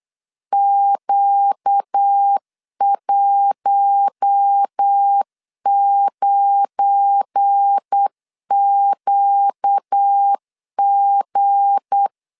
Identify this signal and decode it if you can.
Morse code, heard as a keyed tone, about 8 wpm, 800 Hz. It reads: Q19QG
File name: LEZIONI / 8 YJ45 / 8 YJ4 8wpm 3246